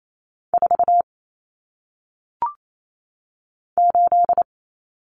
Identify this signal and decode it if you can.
Morse code, heard as a keyed tone, about 28 wpm, 700 Hz. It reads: 4 8
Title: Numbers - 1-Digit 28wpm